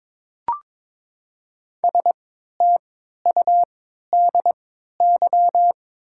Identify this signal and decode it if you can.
Morse code, heard as a keyed tone, about 22 wpm, spaced as for 12 wpm, 700 Hz. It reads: STUDY
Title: 3x Character Spacing - Top 300 Words - 22wpm